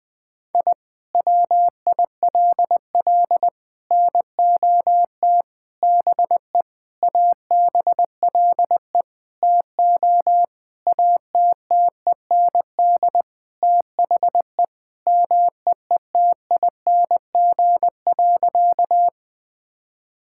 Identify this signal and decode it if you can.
Morse code, heard as a keyed tone, about 20 wpm, 700 Hz. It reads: I WILL NOT BE ABLE TO ATTEND THE MEETING.
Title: Sentences with 5-Word Phrases - Spoken Beginning - 20wpm